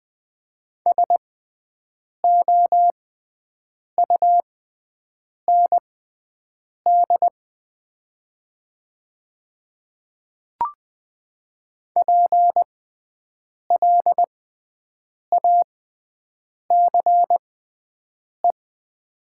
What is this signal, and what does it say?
Morse code, heard as a keyed tone, about 20 wpm, spaced as for 7 wpm, 700 Hz. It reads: SOUND PLACE
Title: 6x Character Spacing - Top 200 Words - 20wpm